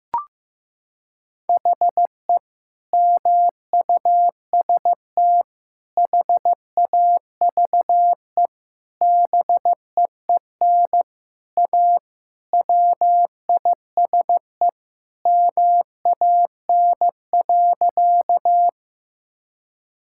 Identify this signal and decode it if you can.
Morse code, heard as a keyed tone, about 15 wpm, 700 Hz. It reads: HE MUST HAVE BEEN A WISE MAN.
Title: Sentences from Top 1600 Words 15wpm